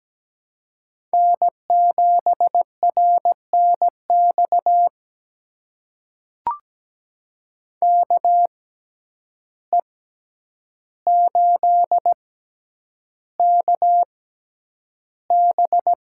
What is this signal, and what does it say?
Morse code, heard as a keyed tone, about 17 wpm, spaced as for 9 wpm, 700 Hz. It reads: N7RNX KE8KB
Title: USA Call Signs - 6x Character Spacing - 17wpm